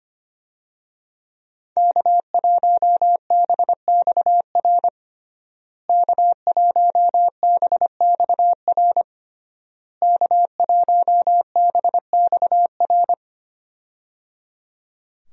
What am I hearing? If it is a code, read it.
Morse code, heard as a keyed tone, about 25 wpm, 700 Hz. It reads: K1BXR K1BXR K1BXR
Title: USA Call Signs - 5 Characters - Repeated 3x - 25wpm